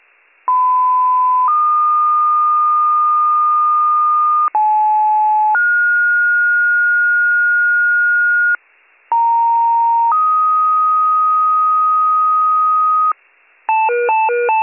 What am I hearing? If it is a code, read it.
Two-tone page, heard as a one-second tone, then a three-second tone, about 1010 then 1220 Hz, about 860 then 1430 Hz, about 930 then 1250 Hz. It is heard from 0.5 to 13.1 s.